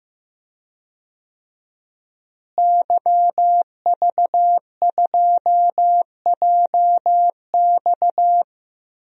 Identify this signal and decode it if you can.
Morse code, heard as a keyed tone, about 15 wpm, 700 Hz. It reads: YV2JX
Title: DXCC-100 Call Signs - 5 Characters - Repeated 3x - 15wpm